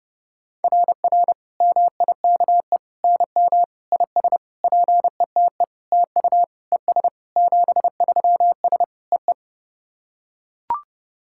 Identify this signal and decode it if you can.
Morse code, heard as a keyed tone, about 30 wpm, 700 Hz. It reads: RR MIKE NM IS PETE TU ES 73S EE
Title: SOTA QSO Practice 30wpm